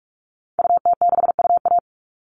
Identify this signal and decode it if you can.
Morse code, heard as a keyed tone, about 44 wpm, 700 Hz. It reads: UT6UA
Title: Call Signs Repeated 3x - 45wpm